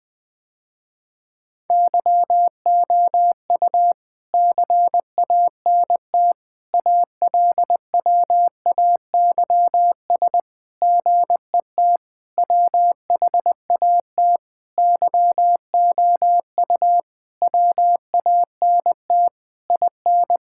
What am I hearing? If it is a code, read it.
Morse code, heard as a keyed tone, about 20 wpm, 700 Hz. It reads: YOU CANT ALWAYS GET WHAT YOU WANT IN
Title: Sentences with 7-Word Phrases - Spoken Beginning - 20wpm